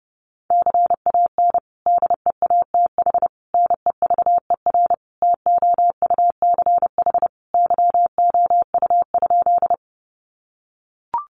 Morse code, heard as a keyed tone, about 30 wpm, 700 Hz. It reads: CAN DEATH NEVER TOUCH YOU?
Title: Sentences from Top 1200 Words 30wpm